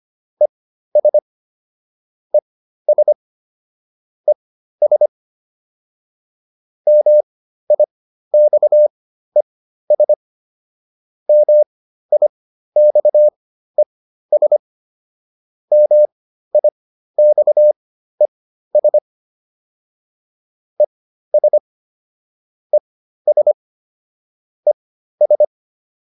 Morse code, heard as a keyed tone, about 25 wpm, spaced as for 13 wpm, 600 Hz. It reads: ES ES ES MIXES MIXES MIXES ES ES ES